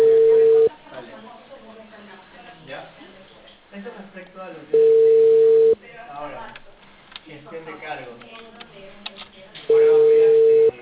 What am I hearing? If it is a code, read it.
Ringback tone (China, 450 Hz), with speech.